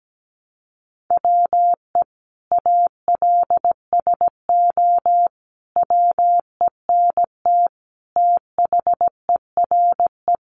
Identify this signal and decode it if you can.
Morse code, heard as a keyed tone, about 17 wpm, 700 Hz. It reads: WE ALSO WENT THERE
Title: Sentences from Top 1200 Words 17wpm